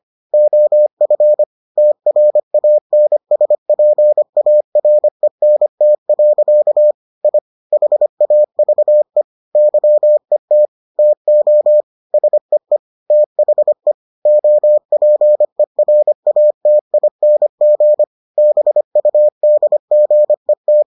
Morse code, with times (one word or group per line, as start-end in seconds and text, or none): 0.3-1.4 OF
1.8-6.9 TRANSPARENT.
7.2-7.4 I
7.7-9.2 HAVE
9.6-10.7 YET
11.0-11.8 TO
12.1-12.8 SEE
13.1-13.9 THE
14.3-18.0 OPERATING
18.4-20.8 BUDGET